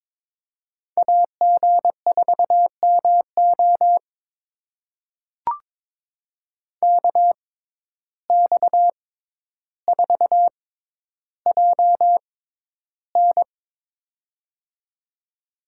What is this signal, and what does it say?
Morse code, heard as a keyed tone, about 22 wpm, spaced as for 11 wpm, 700 Hz. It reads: AG4MO KX4JN
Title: USA Call Signs - 5 Characters - 6x Character Spacing - 22wpm